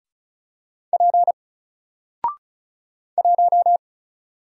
Morse code, heard as a keyed tone, about 36 wpm, 700 Hz. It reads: P 1
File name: Single-Letter-Number-35wpm